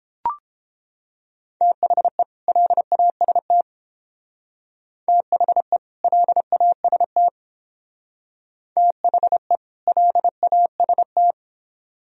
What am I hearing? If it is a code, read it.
Morse code, heard as a keyed tone, about 31 wpm, 700 Hz. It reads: THE LAST THE LAST THE LAST